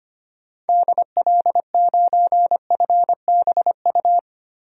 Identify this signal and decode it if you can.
Morse code, heard as a keyed tone, about 25 wpm, 690 Hz. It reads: DL9FBU
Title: DXCC-100 Call Signs - 6 Characters - Repeated 3x - 25wpm